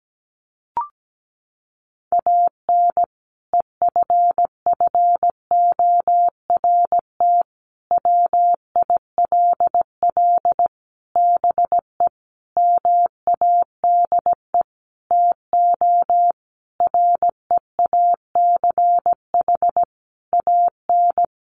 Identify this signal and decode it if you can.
Morse code, heard as a keyed tone, about 17 wpm, 700 Hz. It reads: AN EFFORT WILL BE MADE TO REACH AN
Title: Sentences from Top 1500 Words 17wpm